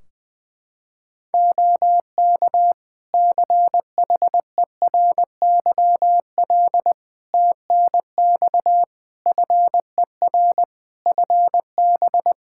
Morse code, heard as a keyed tone, about 20 wpm, 700 Hz. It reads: OK CHERYL TNX FER FB
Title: Practice QSO - Exchange 4 - Ending - 20wpm